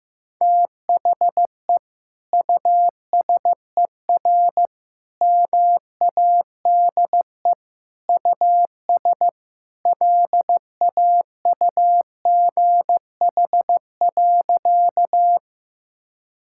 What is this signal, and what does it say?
Morse code, heard as a keyed tone, about 15 wpm, 700 Hz. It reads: THE USER MADE US LAUGH.
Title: Sentences from Top 1000 Words - Review Part 5 - 15wpm